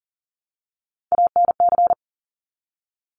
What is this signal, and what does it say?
Morse code, heard as a keyed tone, about 40 wpm, 690 Hz. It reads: ANC
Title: Top Three Letter Combinations in Words 40wpm